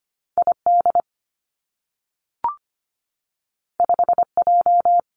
Morse code, heard as a keyed tone, about 25 wpm, 700 Hz. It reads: ID 5J
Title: Two Challenging Letter-Number Combinations 25wpm